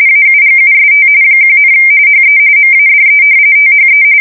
RTTY, about 45.45 baud. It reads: AUGUST 1, HE MADE THE
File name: fsk170 / battleshipnj